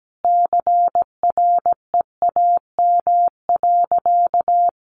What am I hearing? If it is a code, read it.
Morse code, heard as a keyed tone, about 17 wpm, 700 Hz. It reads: CREAM.
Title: Sentences from Top 1500 Words 17wpm